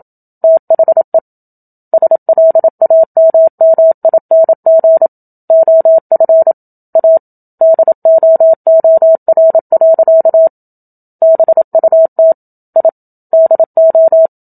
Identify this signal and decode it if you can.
Morse code, heard as a keyed tone, about 27 wpm, 650 Hz. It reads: THE SLAMMING OF A DOOR. BUT I DO
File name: Book Two / 01-Chapter 1 - 27 WPM